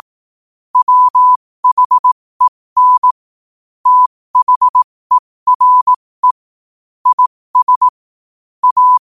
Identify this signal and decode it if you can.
Morse code, heard as a keyed tone, about 18 wpm, spaced as for 15 wpm, 1000 Hz. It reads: WHEN THERE IS A
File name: qotdfiles / 15wpm